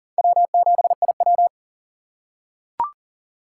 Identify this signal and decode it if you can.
Morse code, heard as a keyed tone, about 40 wpm, 700 Hz. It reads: W7IW
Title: USA Call Signs - 4 Characters - 3x Character Spacing - 40wpm